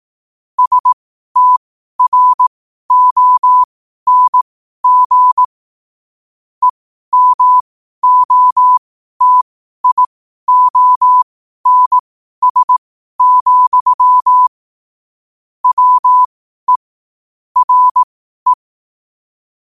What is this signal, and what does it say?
Morse code, heard as a keyed tone, about 18 wpm, spaced as for 12 wpm, 1000 Hz. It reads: STRONG EMOTIONS, WE RE
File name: qotdfiles / 12wpm